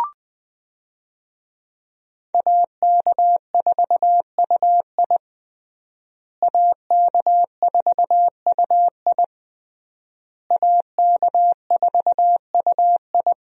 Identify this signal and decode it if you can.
Morse code, heard as a keyed tone, about 20 wpm, 700 Hz. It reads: AK4UI AK4UI AK4UI
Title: USA Call Signs - 5 Characters - Repeated 3x - 20wpm